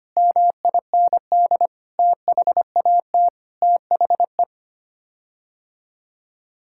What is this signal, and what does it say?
Morse code, heard as a keyed tone, about 25 wpm, 700 Hz. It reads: MIND THAT THE
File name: Sets of 9 Words - 25wpm